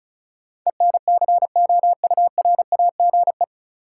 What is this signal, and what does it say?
Morse code, heard as a keyed tone, about 36 wpm, 700 Hz. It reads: ENCOURAGE
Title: Sentences from Top 1400 Words - New Word Drill Alt - 35wpm